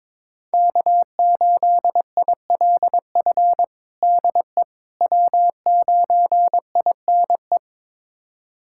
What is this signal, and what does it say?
Morse code, heard as a keyed tone, about 22 wpm, 700 Hz. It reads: K8ILF DE W9INE
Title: USA Callsign DE Callsign - 5 Characters - 22wpm